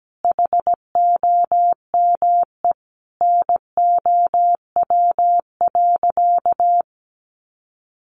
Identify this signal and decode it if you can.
Morse code, heard as a keyed tone, about 17 wpm, 700 Hz. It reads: HOME NOW.
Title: Sentences from Top 1300 Words 17wpm